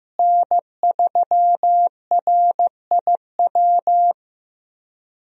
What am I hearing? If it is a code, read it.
Morse code, heard as a keyed tone, about 15 wpm, 700 Hz. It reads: N3RIW
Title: USA Call Signs - 5 Characters - Repeated 3x - 15wpm